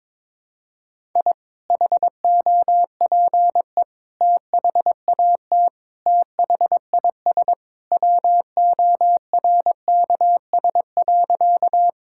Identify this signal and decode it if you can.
Morse code, heard as a keyed tone, about 23 wpm, 700 Hz. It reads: I HOPE THAT THIS WORKS.
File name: practice / 22wpm / Sentences with 2-Word Phrases - 22wpm